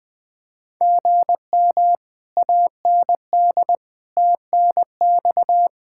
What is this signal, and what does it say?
Morse code, heard as a keyed tone, about 20 wpm, 700 Hz. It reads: GM AND TNX
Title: Speed-Racing Common QSO Phrases Encore - 20wpm